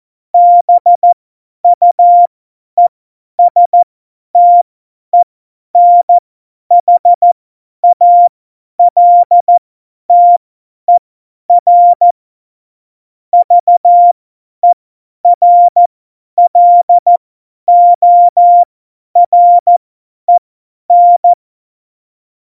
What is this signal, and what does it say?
Morse code, heard as a keyed tone, about 14 wpm, spaced as for 10 wpm, 700 Hz. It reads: BUESTENHALTER VERLOREN